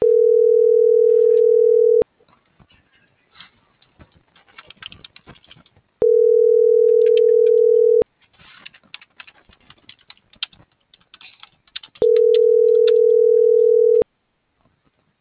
Ringback tone (North America), 440 + 480 Hz.
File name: CENTRAL TERRESTRE / out jessamynvillafuerte